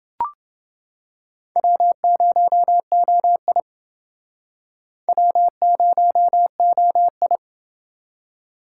Morse code, heard as a keyed tone, about 28 wpm, 700 Hz. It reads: W0OI W0OI